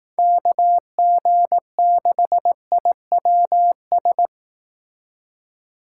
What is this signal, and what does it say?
Morse code, heard as a keyed tone, about 18 wpm, 700 Hz. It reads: KG6IWS
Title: Speed Racing - USA Call Signs 15wpm